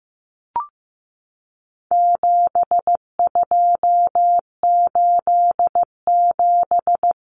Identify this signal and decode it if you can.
Morse code, heard as a keyed tone, about 15 wpm, 700 Hz. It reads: 7287
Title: Common Cut Numbers - 4-Digit 15wpm